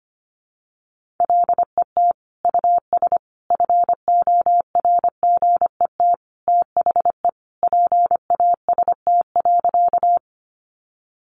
Morse code, heard as a keyed tone, about 25 wpm, 700 Hz. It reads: LET US FORGET THE PAST.
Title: Sentences from Top 1200 Words 25wpm